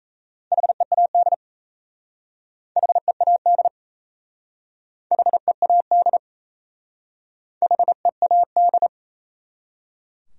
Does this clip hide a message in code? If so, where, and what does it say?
Yes: Morse code from 0.5 to 8.9 s, heard as a keyed tone, about 36 wpm, 700 Hz. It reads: HEAD HEAD HEAD HEAD